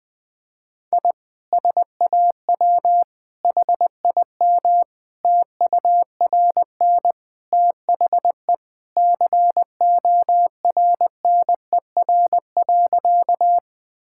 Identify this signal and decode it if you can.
Morse code, heard as a keyed tone, about 20 wpm, 700 Hz. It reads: I SAW HIM TURN THE CORNER.
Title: Sentences from Top 1600 Words 20wpm